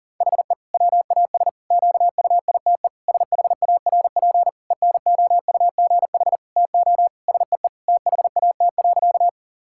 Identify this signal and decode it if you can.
Morse code, heard as a keyed tone, about 40 wpm, 700 Hz. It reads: HE WAS QUITE SHARP ENOUGH TO SEE THAT.
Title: Sentences from Top 1600 Words 40wpm